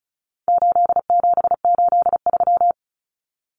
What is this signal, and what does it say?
Morse code, heard as a keyed tone, about 34 wpm, 700 Hz. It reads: 8783
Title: Numbers - 4-Digit 35wpm